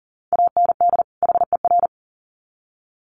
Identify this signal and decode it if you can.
Morse code, heard as a keyed tone, about 40 wpm, 700 Hz. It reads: AND HER
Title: Sets of 2 Words 40wpm